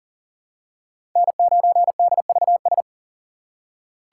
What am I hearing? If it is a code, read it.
Morse code, heard as a keyed tone, about 40 wpm, 700 Hz. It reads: N9DVS
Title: USA Call Signs - 5 Characters - 6x Character Spacing - 40wpm